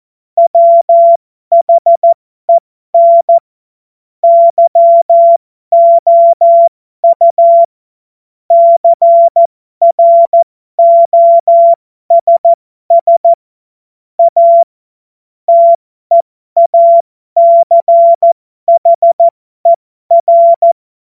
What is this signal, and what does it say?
Morse code, heard as a keyed tone, about 14 wpm, spaced as for 12 wpm, 680 Hz. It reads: WHEN YOU CROSS A TEACHER
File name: Morsekurs / Witze